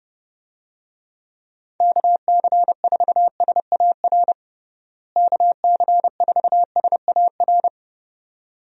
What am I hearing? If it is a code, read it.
Morse code, heard as a keyed tone, about 30 wpm, 700 Hz. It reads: KC4SAR KC4SAR
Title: USA Call Signs - 6 Characters - Repeated 2x - 30wpm